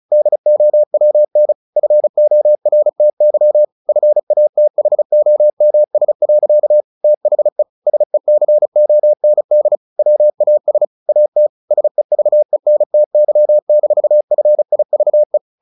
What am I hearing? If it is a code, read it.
Morse code, heard as a keyed tone, about 34 wpm, 600 Hz. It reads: DOWN FORTY FATHOMS. THE SECOND WAS AT SEVENTY-FIVE
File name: the old man and the sea / the old man and the sea 1 35wpm